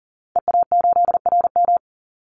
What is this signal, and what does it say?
Morse code, heard as a keyed tone, about 40 wpm, 700 Hz. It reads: EA8RM
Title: Call Signs Repeated 2x - 40wpm